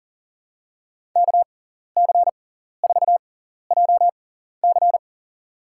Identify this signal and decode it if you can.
Morse code, heard as a keyed tone, about 40 wpm, spaced as for 14 wpm, 700 Hz. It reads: KC4JC
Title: USA Call Signs - 6x Character Spacing - 40wpm